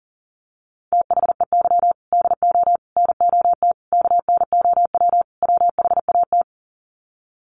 Morse code, heard as a keyed tone, about 40 wpm, 700 Hz. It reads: THEY DO NOT KNOW WHAT
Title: Sets of 5 Words Encore 40wpm